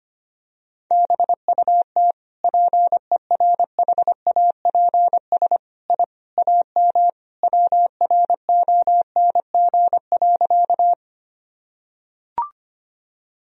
Morse code, heard as a keyed tone, about 25 wpm, 700 Hz. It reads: BUT PERHAPS I AM WRONG.
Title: Sentences from Top 1000 Words - Review Part 2 - 25wpm